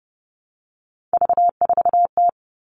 Morse code, heard as a keyed tone, about 30 wpm, 700 Hz. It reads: V4T
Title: Three Challenging Letter-Number Combinations 30wpm